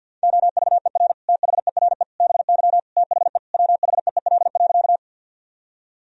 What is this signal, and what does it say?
Morse code, heard as a keyed tone, about 50 wpm, 700 Hz. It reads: OVER THERE BY THE WHEEL.
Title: Sentences from Top 1000 Words - Review Part 3 - 50wpm